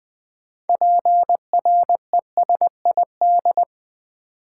Morse code, heard as a keyed tone, about 20 wpm, 700 Hz. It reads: PRESID
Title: Build A Word - 3 Letters At A Time - 9-Letter Words - 20wpm